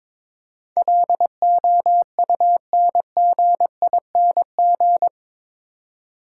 Morse code, heard as a keyed tone, about 24 wpm, 690 Hz. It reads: LOUNGING
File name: Easy Words - Speed-Racing with Hints - 22wpm